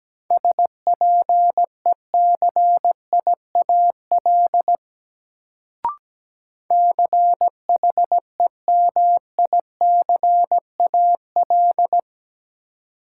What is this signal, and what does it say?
Morse code, heard as a keyed tone, about 17 wpm, 700 Hz. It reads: SPECIAL CHEMICAL
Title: Sentences from Top 1200 Words - New Word Drill Alt - 17wpm